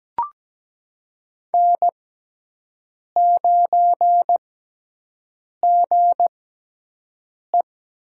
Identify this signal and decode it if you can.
Morse code, heard as a keyed tone, about 17 wpm, spaced as for 6 wpm, 700 Hz. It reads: N9GE
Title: USA Call Signs - 4 Characters - 6x Character Spacing - 17wpm